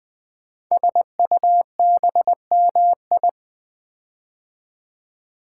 Morse code, heard as a keyed tone, about 20 wpm, 700 Hz. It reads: SUBMI
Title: Build A Word - 5 Letters At A Time - 10-Letter Words - 20wpm